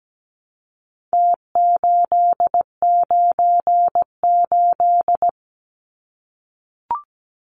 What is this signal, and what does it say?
Morse code, heard as a keyed tone, about 17 wpm, 700 Hz. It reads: T898
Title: Numbers - Contest Style - 4-Digit 17wpm